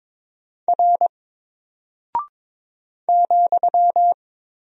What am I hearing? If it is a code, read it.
Morse code, heard as a keyed tone, about 22 wpm, 700 Hz. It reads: R ,
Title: Single Character (Letter:Number:Punctuation:Prosign) Alt 22wpm